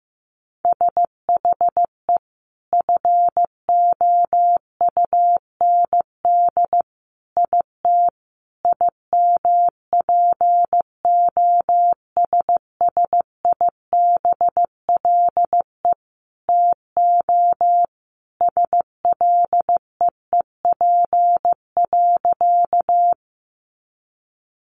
Morse code, heard as a keyed tone, about 15 wpm, 700 Hz. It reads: SHE FOUND IT IMPOSSIBLE TO SLEEP.
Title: Sentences from Top 1100 Words 15wpm